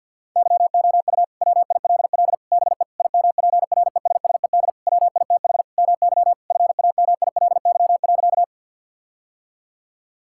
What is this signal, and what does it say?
Morse code, heard as a keyed tone, about 50 wpm, 700 Hz. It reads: YOU WILL BE IMPRESSED WITH MY FAMILY.
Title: Sentences from Top 1600 Words 50wpm